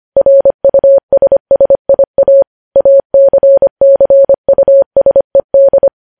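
Morse code, heard as a keyed tone, about 25 wpm, 550 Hz. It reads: RUSSIA ACCUSED
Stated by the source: Morse practice of news headlines at 25wpm.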